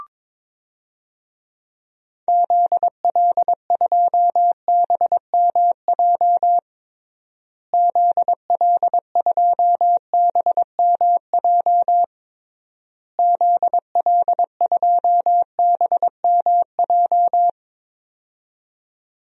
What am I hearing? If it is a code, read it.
Morse code, heard as a keyed tone, about 22 wpm, 700 Hz. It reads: ZL2BMJ ZL2BMJ ZL2BMJ
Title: DXCC-100 Call Signs - 6 Characters - Repeated 3x - 22wpm